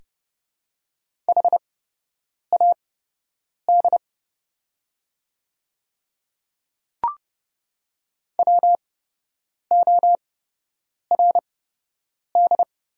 Morse code, heard as a keyed tone, about 31 wpm, spaced as for 8 wpm, 700 Hz. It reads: HAD WORD
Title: 8x Character Spacing - Top 100 Words - 30wpm